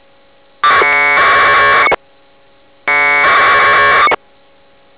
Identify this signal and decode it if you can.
Packet radio (AX.25, as in APRS).